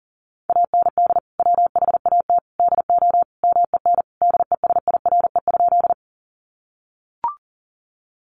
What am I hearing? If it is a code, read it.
Morse code, heard as a keyed tone, about 40 wpm, 700 Hz. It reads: AND WHAT DO MEN DESIRE?
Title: Sentences from Top 1100 Words 40wpm